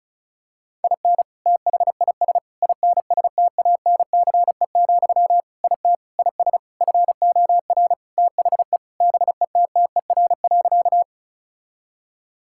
Morse code, heard as a keyed tone, about 36 wpm, 700 Hz. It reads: IN THIS INSTANCE, IT IS FOR THE BETTER.
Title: Sentences from Top 1000 Words - Review Part 5 - 35wpm